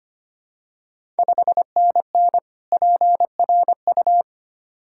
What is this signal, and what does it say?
Morse code, heard as a keyed tone, about 25 wpm, 700 Hz. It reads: 5NN PRU
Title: Ontario QSO Party Exchange 25wpm